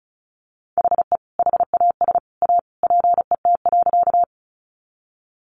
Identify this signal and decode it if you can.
Morse code, heard as a keyed tone, about 34 wpm, 700 Hz. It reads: HE HAS A PET.
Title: Easy Phrases 35wpm